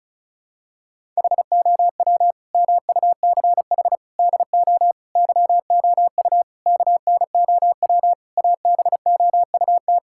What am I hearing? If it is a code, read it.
Morse code, heard as a keyed tone, about 34 wpm, 700 Hz. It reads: HOW MUCH DO YOU KNOW ABOUT
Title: Sentences from Top 1000 Words - Review Part 6 - 35wpm